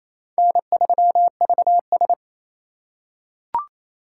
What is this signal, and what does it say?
Morse code, heard as a keyed tone, about 28 wpm, 700 Hz. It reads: N3VS
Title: USA Call Signs - 4 Characters - 2x Character Spacing - 28wpm